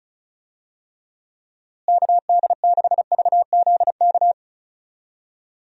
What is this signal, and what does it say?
Morse code, heard as a keyed tone, about 36 wpm, 700 Hz. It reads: KD6VZK
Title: USA Call Signs - 6 Characters - Repeated 3x - 35wpm